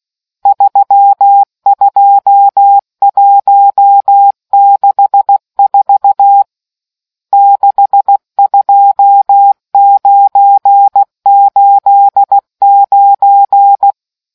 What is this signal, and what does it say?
Morse code, heard as a keyed tone, about 16 wpm, 800 Hz. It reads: 32164 62989